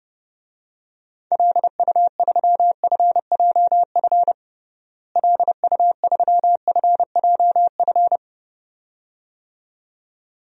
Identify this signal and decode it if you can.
Morse code, heard as a keyed tone, about 30 wpm, 700 Hz. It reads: LU3FJF LU3FJF
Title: DXCC-100 Call Signs - 6 Characters - Repeated 2x - 30wpm